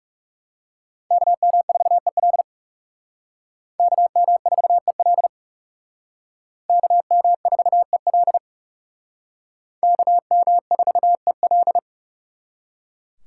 Morse code, heard as a keyed tone, about 37 wpm, 700 Hz. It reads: KM4EL KM4EL KM4EL KM4EL
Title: Speed Racing - USA 2x2 Call Signs 30wpm